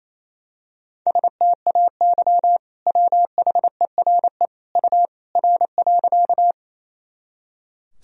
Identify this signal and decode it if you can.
Morse code, heard as a keyed tone, about 28 wpm, 700 Hz. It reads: STAY WHERE U R.
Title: Sentences from Top 500 Words and Abbreviations 28wpm